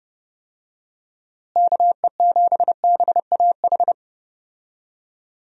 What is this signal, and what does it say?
Morse code, heard as a keyed tone, about 31 wpm, 700 Hz. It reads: KE7BAH
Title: USA Call Signs - 6 Characters - Repeated 2x - 30wpm